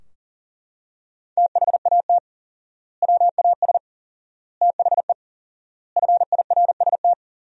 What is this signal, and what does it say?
Morse code, heard as a keyed tone, about 37 wpm, 700 Hz. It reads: THAT WAS THE FIRST
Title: 4x Word-Spacing - Sentences from Top 300 Words - 40wpm